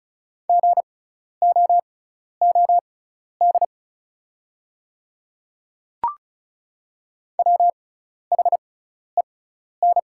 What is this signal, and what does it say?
Morse code, heard as a keyed tone, about 36 wpm, spaced as for 12 wpm, 700 Hz. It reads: GOOD WHEN
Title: 6x Character Spacing - Top 200 Words - 35wpm